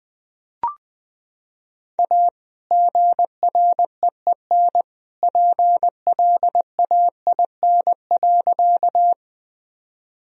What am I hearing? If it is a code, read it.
Morse code, heard as a keyed tone, about 20 wpm, 700 Hz. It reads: A GREEN PLAIN.
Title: Sentences from Top 1000 Words - Review Part 2 - 20wpm